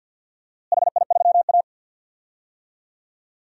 Morse code, heard as a keyed tone, about 48 wpm, 700 Hz. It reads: HI3A
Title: DXCC-100 Call Signs - 4 Characters - Repeated 2x - 50wpm